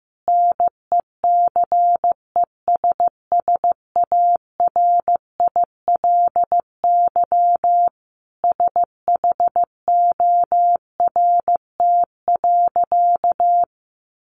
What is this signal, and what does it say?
Morse code, heard as a keyed tone, about 15 wpm, 700 Hz. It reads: NECESSARILY SHORT.
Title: Sentences from Top 1400 Words 15wpm